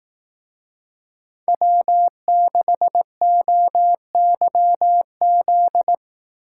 Morse code, heard as a keyed tone, about 18 wpm, 700 Hz. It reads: W6OYZ
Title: Speed Racing - USA 1x3 Call Signs 15wpm